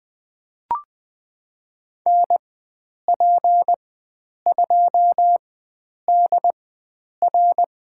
Morse code, heard as a keyed tone, about 20 wpm, spaced as for 9 wpm, 700 Hz. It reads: NP2DR
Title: USA Call Signs - 4x Character Spacing - 20wpm